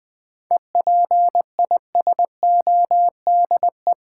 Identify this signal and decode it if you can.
Morse code, heard as a keyed tone, about 20 wpm, 700 Hz. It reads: EPISODE
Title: 7 Letter Words - First 3 Letters Given - 20wpm